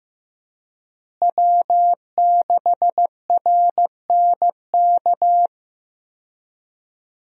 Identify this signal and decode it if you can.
Morse code, heard as a keyed tone, about 15 wpm, 700 Hz. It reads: W6RNK
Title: USA Call Signs - 5 Characters - Repeated 2x - 15wpm